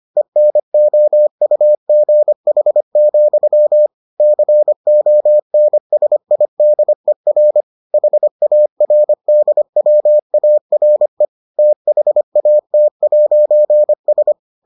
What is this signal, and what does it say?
Morse code, heard as a keyed tone, about 25 wpm, 600 Hz. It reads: ENOUGH, CONSIDER HARDWARE THAT'S